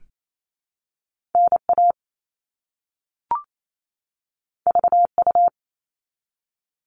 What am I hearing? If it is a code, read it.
Morse code, heard as a keyed tone, about 28 wpm, 700 Hz. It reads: NA VU
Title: All Cut Numbers - 2-Digit 28wpm